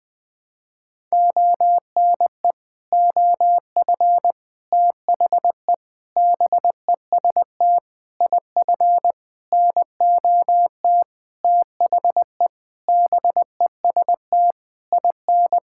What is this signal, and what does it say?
Morse code, heard as a keyed tone, about 20 wpm, 700 Hz. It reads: ONE OF THE BEST IF NOT THE BEST IN
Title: Sentences with 8-Word Phrases - Spoken Beginning - 20wpm